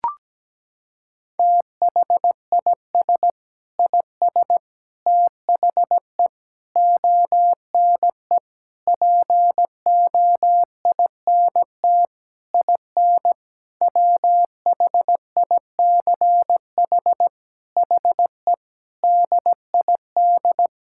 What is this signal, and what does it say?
Morse code, heard as a keyed tone, about 17 wpm, 700 Hz. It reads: THIS IS THE ONE POINT IN WHICH HE DID
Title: Sentences from Top 1000 Words - Review Part 5 - 17wpm